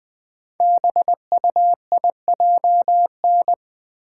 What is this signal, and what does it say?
Morse code, heard as a keyed tone, about 20 wpm, 700 Hz. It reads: BUIJN
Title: Code Groups - 5 Letters - 20wpm